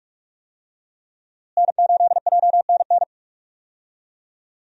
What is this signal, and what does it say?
Morse code, heard as a keyed tone, about 46 wpm, 700 Hz. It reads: N8JNN
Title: USA Call Signs - 5 Characters - Repeated 3x - 45wpm